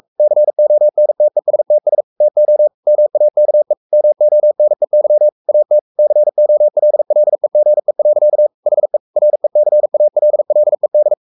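Morse code, heard as a keyed tone, about 44 wpm, 600 Hz. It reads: CONTESTS TO MAKE MONEY AT COLLEGE. HE RECALLED
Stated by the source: Morse code at 43 wpm sending Ken Follett's novel Never in English, from CW med Gnister.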